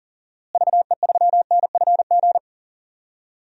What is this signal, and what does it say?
Morse code, heard as a keyed tone, about 40 wpm, 700 Hz. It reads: VE3NFG
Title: DXCC-100 Call Signs - 6 Characters - Repeated 2x - 40wpm